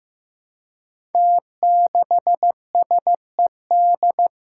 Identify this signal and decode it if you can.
Morse code, heard as a keyed tone, about 15 wpm, 700 Hz. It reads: T6SED
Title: Five Challenging Letter-Number Combinations 15wpm